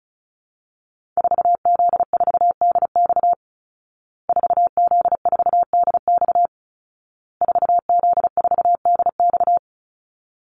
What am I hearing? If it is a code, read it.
Morse code, heard as a keyed tone, about 34 wpm, 700 Hz. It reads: 4Z4DX 4Z4DX 4Z4DX
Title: Call Signs Repeated 2x - 35wpm